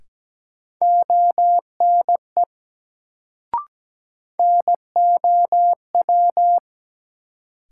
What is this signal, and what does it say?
Morse code, heard as a keyed tone, about 17 wpm, 700 Hz. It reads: ONE NOW
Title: Sentences from Top 100 Words - New Word Drill - 17wpm